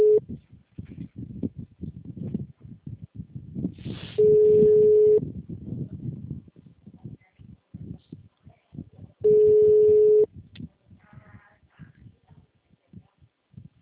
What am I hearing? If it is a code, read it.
Ringback tone (Europe, 425 Hz).